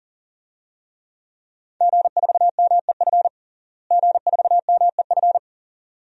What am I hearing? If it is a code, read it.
Morse code, heard as a keyed tone, about 40 wpm, 700 Hz. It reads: G4MEF G4MEF
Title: DXCC-100 Call Signs - 5 Characters - Repeated 2x - 40wpm